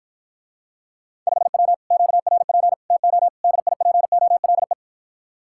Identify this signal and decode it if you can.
Morse code, heard as a keyed tone, about 53 wpm, 700 Hz. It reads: 5W QRP TO DIPOLE